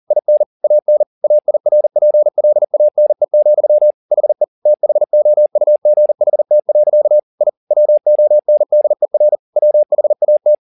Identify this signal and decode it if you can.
Morse code, heard as a keyed tone, about 40 wpm, 600 Hz. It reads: IN AN AIRPLANE, HE THOUGHT. I WONDER WHAT
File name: the old man and the sea 2 40wpm